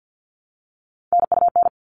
Morse code, heard as a keyed tone, about 50 wpm, 700 Hz. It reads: NUN
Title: All Cut Numbers - 3-Digit 50wpm